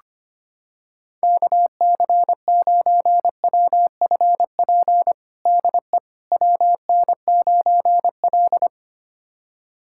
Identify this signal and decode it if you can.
Morse code, heard as a keyed tone, about 25 wpm, 700 Hz. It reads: KC9WFP DE WN9L
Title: USA Callsign DE Callsign 25wpm